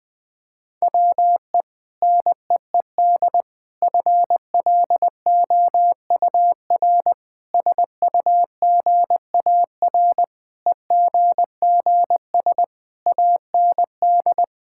Morse code, heard as a keyed tone, about 20 wpm, 700 Hz. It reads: WE NEED FLOUR SUGAR EGGS AND
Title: Sentences with 6-Word Phrases - 20wpm